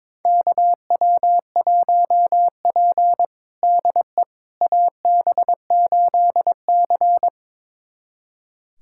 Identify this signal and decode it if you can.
Morse code, heard as a keyed tone, about 22 wpm, 700 Hz. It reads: KW1P DE AB8C
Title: USA Callsign DE Callsign - 4 Characters - 22wpm